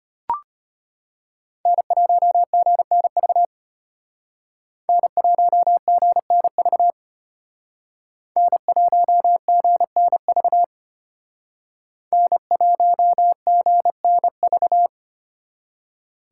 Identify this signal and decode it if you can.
Morse code, heard as a keyed tone, about 33 wpm, 700 Hz. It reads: N1GNV N1GNV N1GNV N1GNV